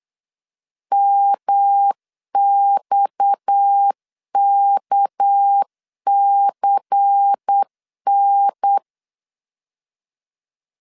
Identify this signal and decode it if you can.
Morse code, heard as a keyed tone, about 8 wpm, 800 Hz. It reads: MXKCN
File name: CW / DET / 1 9 1500C S A 8wpm 5637